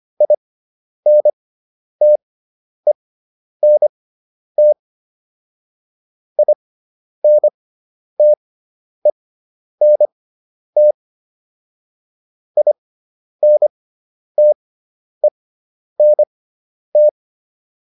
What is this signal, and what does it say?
Morse code, heard as a keyed tone, about 25 wpm, spaced as for 10 wpm, 600 Hz. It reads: INTENT INTENT INTENT